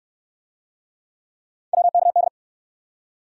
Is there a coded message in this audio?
Yes: Morse code from 1.7 to 2.3 s, heard as a keyed tone, about 63 wpm, 700 Hz. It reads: WKD